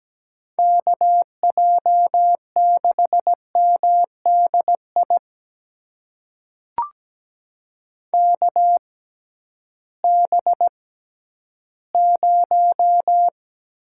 Morse code, heard as a keyed tone, about 17 wpm, spaced as for 11 wpm, 700 Hz. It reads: KJ6MDI KB0
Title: USA Call Signs - 6 Characters - 6x Character Spacing - 17wpm